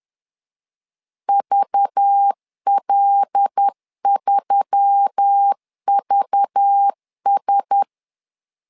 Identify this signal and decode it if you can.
Morse code, heard as a keyed tone, about 10 wpm, 800 Hz. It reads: VL3VS
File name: LEZIONI / 5 LBVC / 5 LBV 10wpm 2248